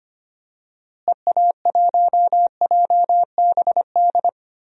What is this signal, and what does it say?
Morse code, heard as a keyed tone, about 25 wpm, 700 Hz. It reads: EA1JBD
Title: DXCC-100 Call Signs - 6 Characters - Repeated 2x - 25wpm